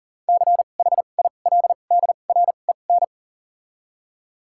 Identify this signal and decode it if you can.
Morse code, heard as a keyed tone, about 39 wpm, spaced as for 28 wpm, 690 Hz. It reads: CHILDREN
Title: 2x Character Spacing - Top 500 Words - 40wpm